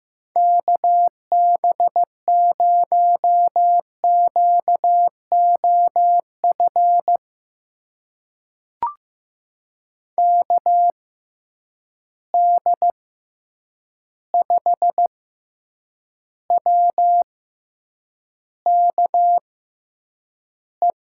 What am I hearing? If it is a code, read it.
Morse code, heard as a keyed tone, about 15 wpm, spaced as for 8 wpm, 700 Hz. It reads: KB0QOF KD5WKE
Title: USA Call Signs - 6 Characters - 6x Character Spacing - 15wpm